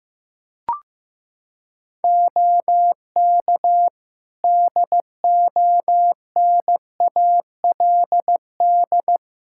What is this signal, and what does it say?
Morse code, heard as a keyed tone, about 15 wpm, 700 Hz. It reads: OK DONALD
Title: Practice QSO - Exchange 3 - More Info - 15wpm